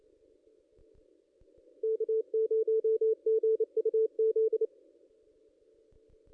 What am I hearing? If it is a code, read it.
Morse code, heard as a keyed tone, about 28 wpm, 440 Hz. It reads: K0GUZ
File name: contests / ss-cw-k5tr-op-n5rz -1900